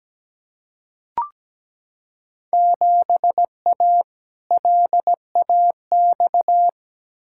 Morse code, heard as a keyed tone, about 17 wpm, 700 Hz. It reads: 7A LAX
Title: Field Day Exchanges Encore - 17wpm